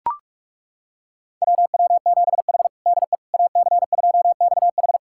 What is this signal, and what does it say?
Morse code, heard as a keyed tone, about 42 wpm, 700 Hz. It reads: WW7H DE AC2XH
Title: USA Callsign DE Callsign 45wpm